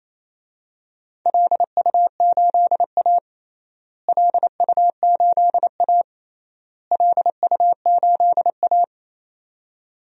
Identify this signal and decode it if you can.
Morse code, heard as a keyed tone, about 28 wpm, 700 Hz. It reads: LU8A LU8A LU8A
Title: DXCC-100 Call Signs - 4 Characters - Repeated 3x - 28wpm